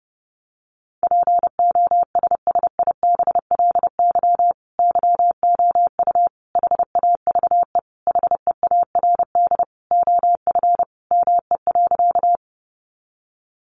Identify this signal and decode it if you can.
Morse code, heard as a keyed tone, about 30 wpm, 700 Hz. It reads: POSSIBLY YOU HAVE HEARD OF ME.
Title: Sentences from Top 1300 Words 30wpm